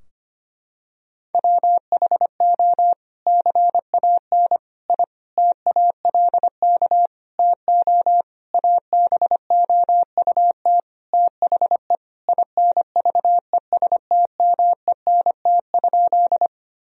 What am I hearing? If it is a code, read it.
Morse code, heard as a keyed tone, about 25 wpm, 700 Hz. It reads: WHO CAN I TALK TO ABOUT THE INVESTMENT?
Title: Sentences from Top 1000 Words - Review Part 4 - 25wpm